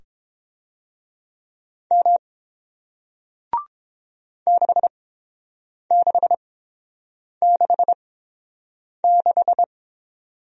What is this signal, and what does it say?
Morse code, heard as a keyed tone, about 30 wpm, spaced as for 7 wpm, 700 Hz. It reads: M 6666